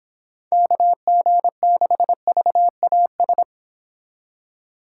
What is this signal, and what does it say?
Morse code, heard as a keyed tone, about 26 wpm, 700 Hz. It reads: KG6VAS
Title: Speed Racing - USA Call Signs 22wpm